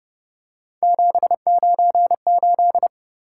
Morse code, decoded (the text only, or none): 798